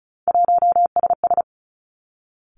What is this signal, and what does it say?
Morse code, heard as a keyed tone, about 36 wpm, 700 Hz. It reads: 1SS
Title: Three Challenging Letter-Number Combinations 35wpm